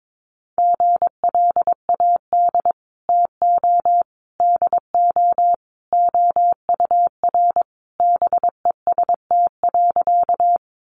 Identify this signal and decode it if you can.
Morse code, heard as a keyed tone, about 22 wpm, 700 Hz. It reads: GLAD TO DO OUR BEST.
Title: Sentences from Top 800 Words 22wpm